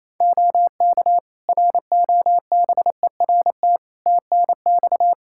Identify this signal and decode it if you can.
Morse code, heard as a keyed tone, about 28 wpm, 700 Hz. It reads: OK ROBERT TNX
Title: Practice QSO - Exchange 4 - Ending - 28wpm